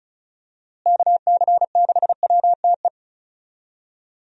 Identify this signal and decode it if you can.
Morse code, heard as a keyed tone, about 34 wpm, 700 Hz. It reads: KC6WTE
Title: USA Call Signs - 6 Characters - Repeated 2x - 35wpm